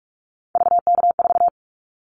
Morse code, heard as a keyed tone, about 44 wpm, 700 Hz. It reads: VK4
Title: Callsign Prefixes for Top 100 DXCC - 45wpm